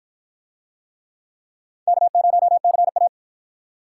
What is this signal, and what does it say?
Morse code, heard as a keyed tone, about 44 wpm, 700 Hz. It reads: K0CA